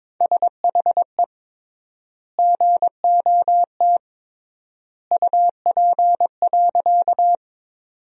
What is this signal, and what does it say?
Morse code, heard as a keyed tone, about 22 wpm, 700 Hz. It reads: SHE GOT UP.
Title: 3x Word-Spacing - Sentences from Top 300 Words - 22wpm